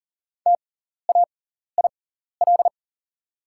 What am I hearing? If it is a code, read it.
Morse code, heard as a keyed tone, about 40 wpm, spaced as for 14 wpm, 700 Hz. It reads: TAIL
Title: 6x Character Spacing - Top 500 Words - 40wpm